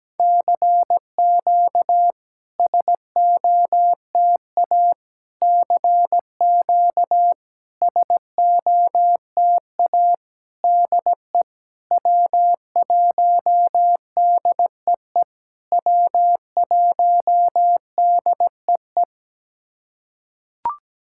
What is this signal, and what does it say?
Morse code, heard as a keyed tone, about 17 wpm, 700 Hz. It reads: CQ SOTA CQ SOTA DE W1DEE W1DEE
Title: SOTA QSO Practice 17wpm